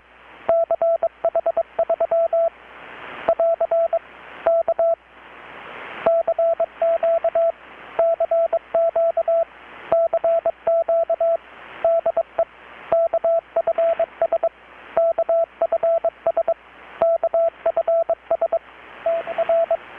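Morse code, heard as a keyed tone, about 22 wpm, 670 Hz, with radio noise. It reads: CH3 <AR> K CQ CQ CQ DE KFS KFS KFS TR
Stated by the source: Morse from KPH's Night of Nights XX maritime radiotelegraphy event, received in Chiapas, Mexico about 3,800 km away on a Kenwood R-600 with a grounded 10 m long-wire antenna.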